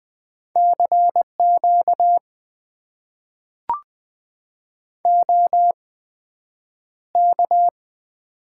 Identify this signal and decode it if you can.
Morse code, heard as a keyed tone, about 20 wpm, spaced as for 9 wpm, 700 Hz. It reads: CQ OK
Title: Top 90 QSO Elements - 8x Character Spacing - 20wpm